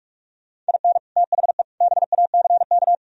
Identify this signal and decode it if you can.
Morse code, heard as a keyed tone, about 58 wpm, 700 Hz. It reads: IN THE BACK